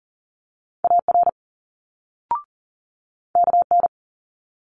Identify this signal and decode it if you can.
Morse code, heard as a keyed tone, about 42 wpm, 700 Hz. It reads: AR KN
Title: Multi-letter Phonemes Quick Fire 40wpm